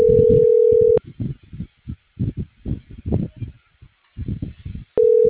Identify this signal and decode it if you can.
Ringback tone (North America), 440 + 480 Hz.